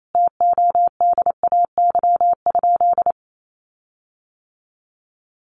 Morse code, heard as a keyed tone, about 28 wpm, 700 Hz. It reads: TODAY?
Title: Sentences from Top 1500 Words 28wpm